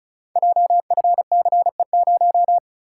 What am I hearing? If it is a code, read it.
Morse code, heard as a keyed tone, about 36 wpm, 700 Hz. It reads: JFCE0
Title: Code Groups - 5 Letter:Numbers - 35wpm